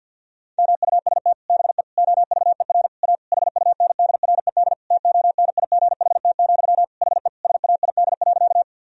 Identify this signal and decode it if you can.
Morse code, heard as a keyed tone, about 52 wpm, 700 Hz. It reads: MUST BE OVER A HUNDRED TONIGHT, HE SAID.
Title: Sentences from Top 1000 Words - Review Part 2 - 50wpm